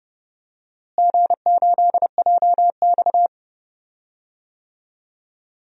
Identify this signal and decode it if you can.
Morse code, heard as a keyed tone, about 32 wpm, 700 Hz. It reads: G8JX
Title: DXCC-100 Call Signs - 4 Characters - Repeated 2x - 30wpm